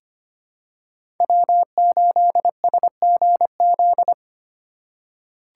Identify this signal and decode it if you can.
Morse code, heard as a keyed tone, about 25 wpm, 700 Hz. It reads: W8SGZ
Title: USA Call Signs - 5 Characters - 3x Character Spacing - 25wpm